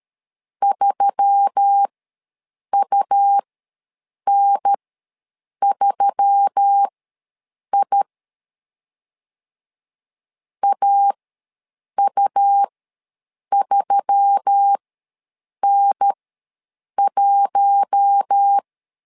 Morse code, heard as a keyed tone, about 13 wpm, spaced as for 7 wpm, 800 Hz. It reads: 3UN3I AU3N1
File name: LEZIONI / 4 Q1S3 ws 12wpm 2038